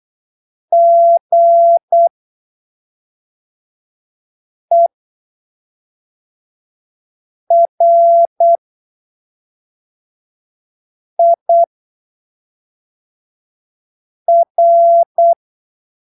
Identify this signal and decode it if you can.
Morse code, heard as a keyed tone, about 8 wpm, spaced as for 3 wpm, 680 Hz. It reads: GERIR
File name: arquivos / texto / texto 06 08ppm